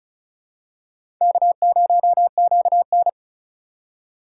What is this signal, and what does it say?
Morse code, heard as a keyed tone, about 36 wpm, 700 Hz. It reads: K0QN
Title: USA Call Signs - 4 Characters - 2x Character Spacing - 35wpm